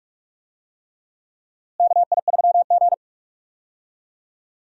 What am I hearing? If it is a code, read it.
Morse code, heard as a keyed tone, about 46 wpm, 700 Hz. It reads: KI3G
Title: USA Call Signs - 4 Characters - Repeated 3x - 45wpm